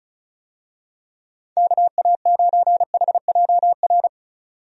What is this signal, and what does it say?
Morse code, heard as a keyed tone, about 36 wpm, 700 Hz. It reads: KA9HJR
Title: USA Call Signs - 6 Characters - 35wpm